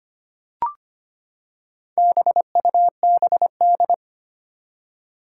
Morse code, heard as a keyed tone, about 25 wpm, 700 Hz. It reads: BUBD
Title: Four Challenging Letter-Number Combinations 25wpm